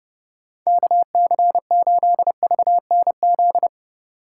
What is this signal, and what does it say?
Morse code, heard as a keyed tone, about 32 wpm, 700 Hz. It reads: KC8VNZ
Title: USA Call Signs - 6 Characters - 2x Character Spacing - 30wpm